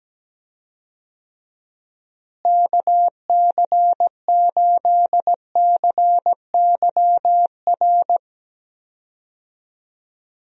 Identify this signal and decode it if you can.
Morse code, heard as a keyed tone, about 17 wpm, 700 Hz. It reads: KC8CYR
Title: USA Call Signs - 6 Characters - Repeated 3x - 17wpm